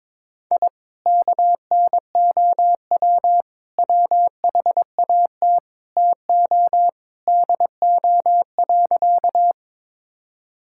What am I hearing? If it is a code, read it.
Morse code, heard as a keyed tone, about 22 wpm, 700 Hz. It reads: I KNOW WHAT TO DO.
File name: practice / 22wpm / Sentences with 3-Word Phrases - 22wpm